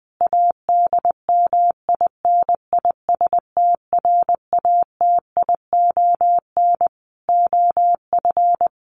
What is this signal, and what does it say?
Morse code, heard as a keyed tone, about 20 wpm, 700 Hz. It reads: ADMINISTRATION OF
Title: Sentences from Top 1300 Words 20wpm